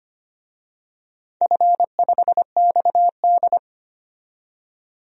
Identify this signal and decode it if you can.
Morse code, heard as a keyed tone, about 25 wpm, 700 Hz. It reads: F5XD
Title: DXCC-100 Call Signs - 4 Characters - 25wpm